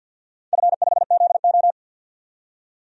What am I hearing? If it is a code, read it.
Morse code, heard as a keyed tone, about 52 wpm, 700 Hz. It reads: F5ZO
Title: DXCC-100 Call Signs - 4 Characters - 50wpm